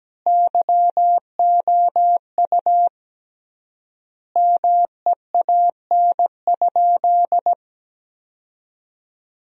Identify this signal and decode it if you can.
Morse code, heard as a keyed tone, about 17 wpm, 700 Hz. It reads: YOU MEAN?
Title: 3x Word-Spacing - Sentences from Top 200 Words - 17wpm